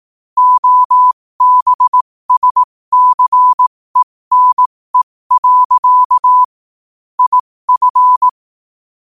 Morse code, heard as a keyed tone, about 18 wpm, spaced as for 15 wpm, 1000 Hz. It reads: OBSCENE. IF
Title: 15 WPM morse code quotes for Fri, 01 Mar 2024